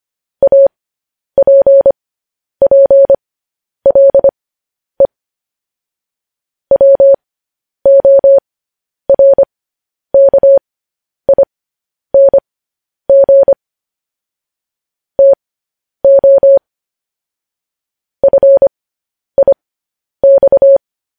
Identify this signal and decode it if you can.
Morse code, heard as a keyed tone, about 25 wpm, spaced as for 10 wpm, 550 Hz. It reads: APPLE WORKING TO FIX